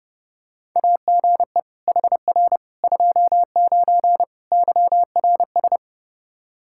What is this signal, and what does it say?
Morse code, heard as a keyed tone, about 30 wpm, 700 Hz. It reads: AGE HR 29 YRS
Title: Practice QSO - Exchange 3 - More Info - 30wpm